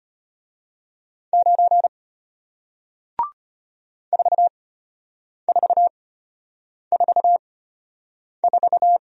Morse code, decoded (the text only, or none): 9 4444